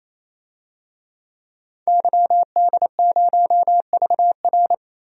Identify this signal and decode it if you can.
Morse code, heard as a keyed tone, about 28 wpm, 700 Hz. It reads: YD0VR
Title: DXCC-100 Call Signs - 5 Characters - 28wpm